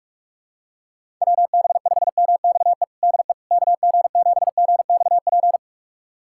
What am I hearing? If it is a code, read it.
Morse code, heard as a keyed tone, about 46 wpm, 700 Hz. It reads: WB5MXE DE KG7GXP
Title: USA Callsign DE Callsign - 6 Characters - 45wpm